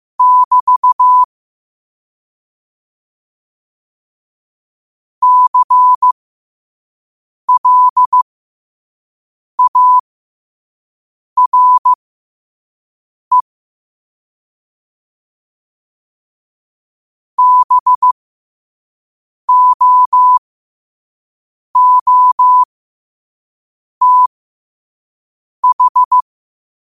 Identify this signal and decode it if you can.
Morse code, heard as a keyed tone, about 15 wpm, spaced as for 5 wpm, 1000 Hz. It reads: <BT> CLARE BOOTH
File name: qotdfiles / 5wpm